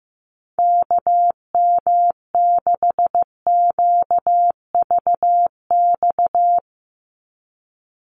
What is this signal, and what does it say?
Morse code, heard as a keyed tone, about 15 wpm, 700 Hz. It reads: KM6QVX
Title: USA Call Signs 15wpm